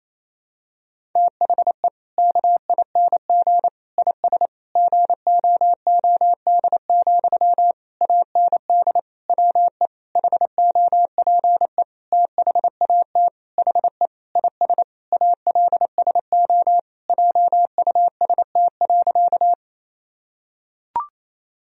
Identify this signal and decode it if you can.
Morse code, heard as a keyed tone, about 27 wpm, 700 Hz. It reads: THE KING IS GOOD, AND WE HOPE THAT HE IS ALSO JUST.
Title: Sentences from Top 1000 Words - Review Part 1 - 28wpm